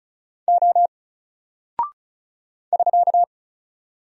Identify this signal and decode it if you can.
Morse code, heard as a keyed tone, about 35 wpm, 700 Hz. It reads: O <SK>
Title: Single Character (Letter:Number:Punctuation:Prosign) Alt 35wpm